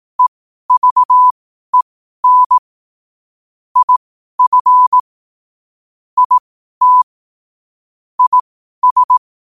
Morse code, heard as a keyed tone, about 18 wpm, spaced as for 12 wpm, 1000 Hz. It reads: EVEN IF IT IS
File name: qotdfiles / 12wpm